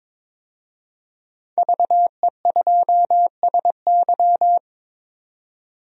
Morse code, heard as a keyed tone, about 22 wpm, 700 Hz. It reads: VE2SY
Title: DXCC-100 Call Signs - 5 Characters - 22wpm